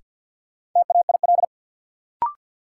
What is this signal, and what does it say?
Morse code, heard as a keyed tone, about 50 wpm, 700 Hz. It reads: TAIL
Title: Sentences from Top 1000 Words - Review Part 3 - New Word Drill - 50wpm